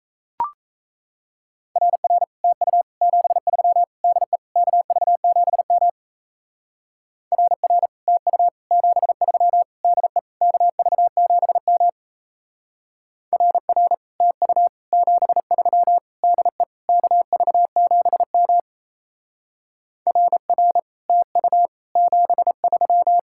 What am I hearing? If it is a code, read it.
Morse code, heard as a keyed tone, about 35 wpm, 700 Hz. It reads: RR TU 73 DE KV7M RR TU 73 DE KV7M RR TU 73 DE KV7M RR TU 73
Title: Speed-Racing Common QSO Phrases Encore - 28wpm